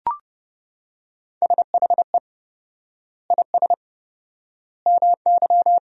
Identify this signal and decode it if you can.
Morse code, heard as a keyed tone, about 30 wpm, 700 Hz. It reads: SHE IS MY
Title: 4x Word-Spacing - Sentences from Top 300 Words - 30wpm